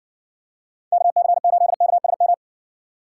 Morse code, heard as a keyed tone, about 60 wpm, 700 Hz. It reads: KC8KSM